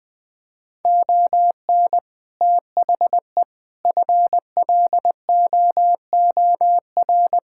Morse code, heard as a keyed tone, about 20 wpm, 700 Hz. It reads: ON THE FLOOR
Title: Build A Phrase - 5-Word Phrases - 20wpm